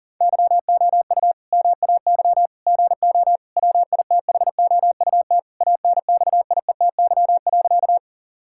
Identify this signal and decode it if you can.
Morse code, heard as a keyed tone, about 40 wpm, 700 Hz. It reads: YOU MAY GO WITHOUT ANXIETY.
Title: Sentences from Top 1600 Words 40wpm